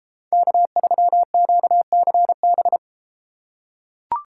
Morse code, heard as a keyed tone, about 32 wpm, 700 Hz. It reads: K3QCB